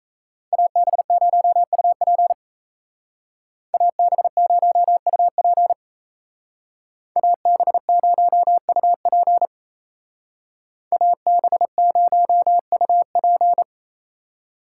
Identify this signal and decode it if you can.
Morse code, heard as a keyed tone, about 36 wpm, 700 Hz. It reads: AB0UP AB0UP AB0UP AB0UP